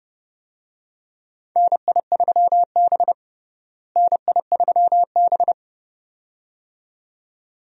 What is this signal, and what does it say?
Morse code, heard as a keyed tone, about 30 wpm, 700 Hz. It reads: NI3B NI3B
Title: USA Call Signs - 4 Characters - Repeated 2x - 30wpm